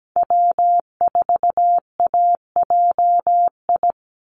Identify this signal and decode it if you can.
Morse code, heard as a keyed tone, about 17 wpm, 700 Hz. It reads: W4AJI
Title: USA Call Signs 17wpm